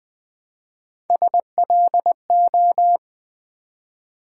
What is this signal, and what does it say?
Morse code, heard as a keyed tone, about 20 wpm, 700 Hz. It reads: SLO
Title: Build A Word - 1 Letter At A Time - 4-Letter Words - 20wpm